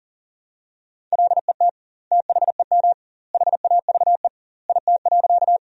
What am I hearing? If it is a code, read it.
Morse code, heard as a keyed tone, about 40 wpm, 700 Hz. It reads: LET THEM HAVE IT.
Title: 2x Word-Spacing - Sentences from Top 300 Words - 40wpm